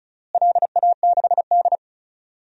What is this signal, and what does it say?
Morse code, heard as a keyed tone, about 36 wpm, 700 Hz. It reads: LA6D
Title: DXCC-100 Call Signs - 4 Characters - Repeated 3x - 35wpm